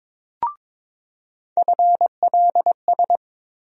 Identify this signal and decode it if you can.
Morse code, heard as a keyed tone, about 22 wpm, 700 Hz. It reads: FLS
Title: USA Call Sign Postfixes - 3 Letters 22wpm